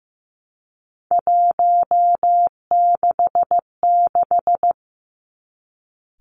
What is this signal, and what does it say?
Morse code, heard as a keyed tone, about 15 wpm, 700 Hz. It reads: 166
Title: Three Challenging Letter-Number Combinations 15wpm